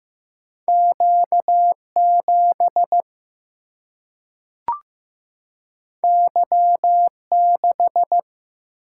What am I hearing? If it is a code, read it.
Morse code, heard as a keyed tone, about 15 wpm, 700 Hz. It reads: Q7 Y6
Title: Single-Letter Single-Number 15wpm